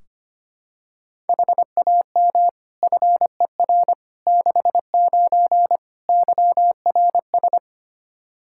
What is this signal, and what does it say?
Morse code, heard as a keyed tone, about 25 wpm, 700 Hz. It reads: HAM FER 69 YRS
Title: K1USN SST QSO Practice 25wpm